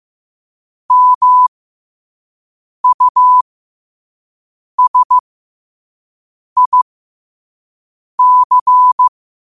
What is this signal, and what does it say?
Morse code, heard as a keyed tone, about 15 wpm, spaced as for 5 wpm, 1000 Hz. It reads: MUSIC
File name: qotdfiles / 5wpm